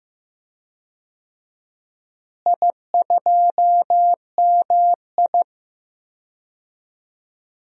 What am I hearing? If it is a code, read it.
Morse code, heard as a keyed tone, about 15 wpm, 700 Hz. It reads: I2MI
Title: DXCC-100 Call Signs - 4 Characters - Repeated 2x - 15wpm